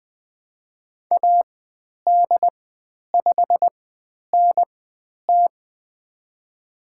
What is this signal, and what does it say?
Morse code, heard as a keyed tone, about 20 wpm, spaced as for 10 wpm, 700 Hz. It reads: AD5NT
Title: 095 Lesson 16g - Letter c - Call Signs 20wpm